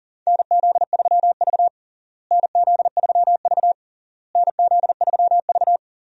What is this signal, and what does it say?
Morse code, heard as a keyed tone, about 40 wpm, 700 Hz. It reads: NZ3V NZ3V NZ3V
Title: USA Call Signs - 4 Characters - Repeated 3x - 40wpm